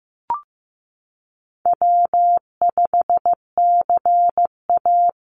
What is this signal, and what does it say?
Morse code, heard as a keyed tone, about 15 wpm, 700 Hz. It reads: W5CA
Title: USA Call Signs 15wpm